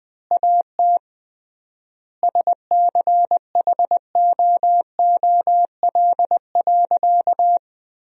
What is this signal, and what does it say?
Morse code, heard as a keyed tone, about 20 wpm, 700 Hz. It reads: AT SCHOOL.
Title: 3x Word-Spacing - Sentences from Top 300 Words - 20wpm